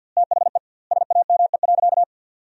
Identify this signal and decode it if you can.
Morse code, heard as a keyed tone, about 50 wpm, 700 Hz. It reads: THE SAME.
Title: 2x Word-Spacing - Sentences from Top 300 Words - 50wpm